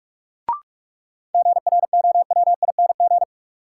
Morse code, heard as a keyed tone, about 42 wpm, 700 Hz. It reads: GROWING
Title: Sets of 5 Words Encore - New Word Drill - 45wpm